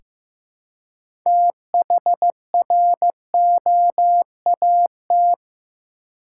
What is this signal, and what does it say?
Morse code, heard as a keyed tone, about 15 wpm, 700 Hz. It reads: THROAT
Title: Six Letter Words - 15wpm